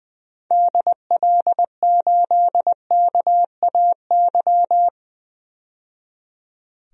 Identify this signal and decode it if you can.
Morse code, heard as a keyed tone, about 20 wpm, 700 Hz. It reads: DL8KAY
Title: DXCC-100 Call Signs - 6 Characters - 20wpm